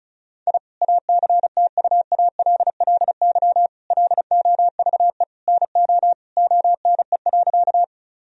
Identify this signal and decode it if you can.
Morse code, heard as a keyed tone, about 36 wpm, 700 Hz. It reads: I ACTUALLY LOVE NO ONE.
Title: Sentences from Top 1000 Words - Review Part 4 - 35wpm